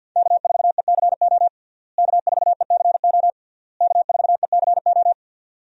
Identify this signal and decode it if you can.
Morse code, heard as a keyed tone, about 52 wpm, 700 Hz. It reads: K4ECO K4ECO K4ECO
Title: USA Call Signs - 5 Characters - Repeated 3x - 50wpm